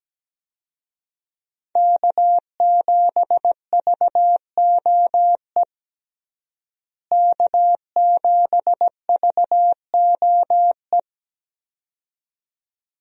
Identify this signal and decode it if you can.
Morse code, heard as a keyed tone, about 17 wpm, 700 Hz. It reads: K7VOE K7VOE
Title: USA Call Signs - 5 Characters - Repeated 2x - 17wpm